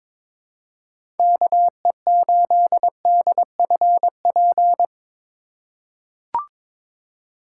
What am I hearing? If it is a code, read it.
Morse code, heard as a keyed tone, about 22 wpm, 700 Hz. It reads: KE8DFP
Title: USA Call Signs - 6 Characters - 3x Character Spacing - 22wpm